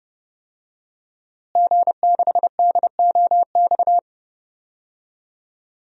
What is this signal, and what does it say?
Morse code, heard as a keyed tone, about 30 wpm, 700 Hz. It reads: G6DOX
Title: DXCC-100 Call Signs - 5 Characters - Repeated 3x - 30wpm